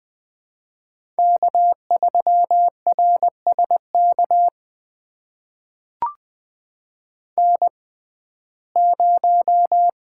Morse code, heard as a keyed tone, about 20 wpm, spaced as for 14 wpm, 700 Hz. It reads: K3RSK N0
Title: USA Call Signs - 5 Characters - 6x Character Spacing - 20wpm